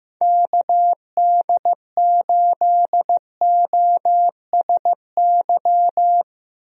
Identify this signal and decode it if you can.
Morse code, heard as a keyed tone, about 15 wpm, 700 Hz. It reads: KD8OSY
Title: USA Call Signs - 6 Characters - 15wpm